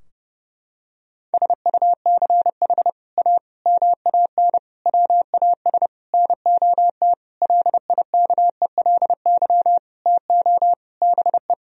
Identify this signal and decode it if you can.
Morse code, heard as a keyed tone, about 32 wpm, 700 Hz. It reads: SUCH A MAN WAS NOT LIKELY TO BE
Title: Sentences from Top 1000 Words - Review Part 4 - 30wpm